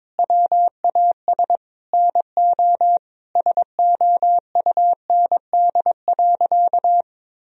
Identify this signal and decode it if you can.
Morse code, heard as a keyed tone, about 22 wpm, 700 Hz. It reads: WAS NO SOUND.
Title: Sentences from Top 1000 Words - Review Part 2 - 22wpm